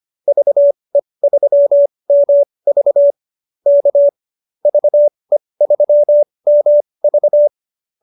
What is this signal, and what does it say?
Morse code, heard as a keyed tone, about 25 wpm, spaced as for 20 wpm, 590 Hz. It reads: VE3MV K VE3MV
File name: qso220b 20